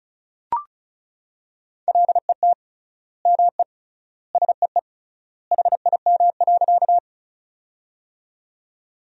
Morse code, heard as a keyed tone, about 34 wpm, 700 Hz. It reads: LET ME SEE HIM.
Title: 3x Word-Spacing - Sentences from Top 300 Words - 35wpm